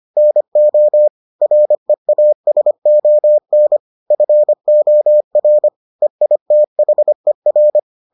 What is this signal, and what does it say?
Morse code, heard as a keyed tone, about 25 wpm, 600 Hz. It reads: NO REASON FOR EITHER